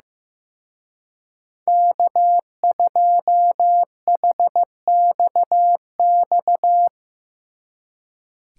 Morse code, heard as a keyed tone, about 16 wpm, 700 Hz. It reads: K2HXX
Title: Speed Racing - USA 1x3 Call Signs 15wpm